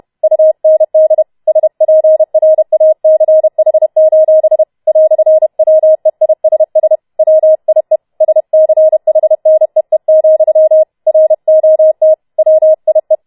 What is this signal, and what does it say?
Morse code, heard as a keyed tone, about 31 wpm, 630 Hz. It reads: UND SPRACH: "WEISS WIE SCHNEE, ROT WIE